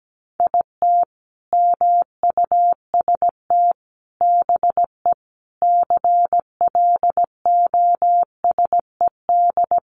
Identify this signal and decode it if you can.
Morse code, heard as a keyed tone, about 17 wpm, 700 Hz. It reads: IT MUST BE CLOSED
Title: Sentences from Top 1400 Words 17wpm